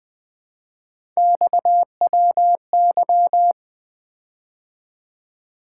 Morse code, heard as a keyed tone, about 20 wpm, 700 Hz. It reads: XWY
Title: Three Letter-Number Combinations - 20wpm